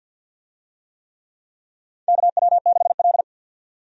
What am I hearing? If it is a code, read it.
Morse code, heard as a keyed tone, about 50 wpm, 700 Hz. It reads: KW6L
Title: USA Call Signs - 4 Characters - 50wpm